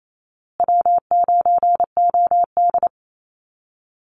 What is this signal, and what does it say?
Morse code, heard as a keyed tone, about 28 wpm, 700 Hz. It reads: W9OD
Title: Call Signs Repeated 2x - 28wpm